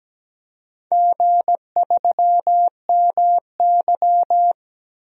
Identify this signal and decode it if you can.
Morse code, heard as a keyed tone, about 17 wpm, 700 Hz. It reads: G3MY
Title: DXCC-100 Call Signs - 4 Characters - 17wpm